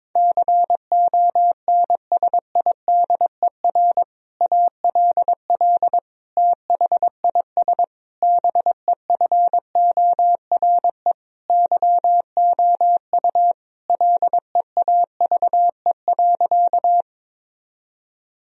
Morse code, heard as a keyed tone, about 22 wpm, 700 Hz. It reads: CONSIDER ALL THIS BEFORE YOU LEAVE.
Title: Sentences from Top 1000 Words - Review Part 4 - 22wpm